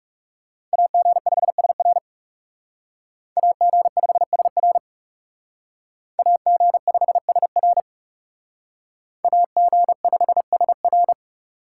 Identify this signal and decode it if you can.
Morse code, heard as a keyed tone, about 38 wpm, 700 Hz. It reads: AG5SR AG5SR AG5SR AG5SR